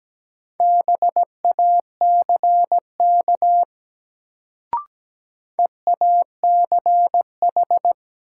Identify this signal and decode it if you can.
Morse code, heard as a keyed tone, about 17 wpm, 700 Hz. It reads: BACK EACH
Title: Sentences from Top 200 Words - New Word Drill - 17wpm